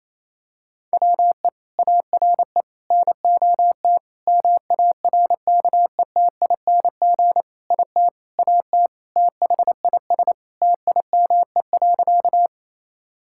Morse code, heard as a keyed tone, about 28 wpm, 700 Hz. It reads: WE ARE NOT MARKETING IT AT THIS TIME.
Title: Sentences from Top 1000 Words - Review Part 5 - 28wpm